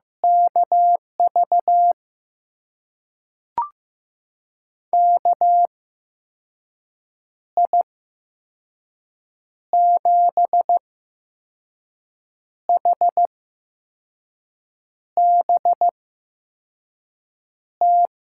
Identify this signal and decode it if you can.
Morse code, heard as a keyed tone, about 15 wpm, spaced as for 5 wpm, 700 Hz. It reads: KV KI7HBT
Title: USA Call Signs - 6 Characters - 8x Character Spacing - 15wpm